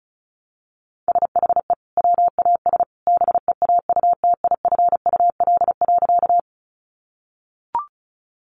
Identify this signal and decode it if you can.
Morse code, heard as a keyed tone, about 34 wpm, 700 Hz. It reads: SHE WAS BEAUTIFUL.
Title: Sentences from Top 900 Words 35wpm